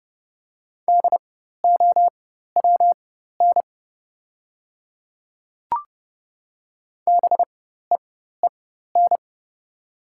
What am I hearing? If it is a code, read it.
Morse code, heard as a keyed tone, about 30 wpm, spaced as for 14 wpm, 700 Hz. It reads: DOWN BEEN
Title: 4x Character Spacing - Top 100 Words - 30wpm